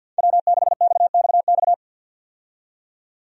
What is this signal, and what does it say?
Morse code, heard as a keyed tone, about 52 wpm, 700 Hz. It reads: W6XXX
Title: USA Call Signs - 5 Characters - 50wpm